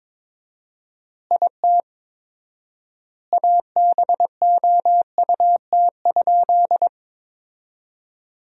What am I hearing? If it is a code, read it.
Morse code, heard as a keyed tone, about 22 wpm, 700 Hz. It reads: IT ABOUT?
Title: 4x Word-Spacing - Sentences from Top 100 Words - 22wpm